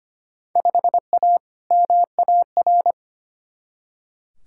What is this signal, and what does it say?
Morse code, heard as a keyed tone, about 25 wpm, 700 Hz. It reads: 5A MAR
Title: Field Day Exchanges Encore - 25wpm